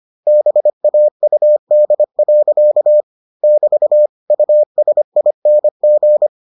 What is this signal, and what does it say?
Morse code, heard as a keyed tone, about 25 wpm, 600 Hz. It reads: BAUD. <BT> USING